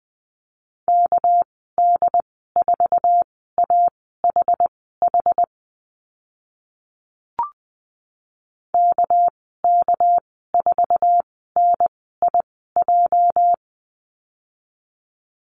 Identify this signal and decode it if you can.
Morse code, heard as a keyed tone, about 20 wpm, spaced as for 15 wpm, 700 Hz. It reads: KD4AHH KK4NIJ
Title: 245 Lesson 39g - Letter z - Call Signs 20wpm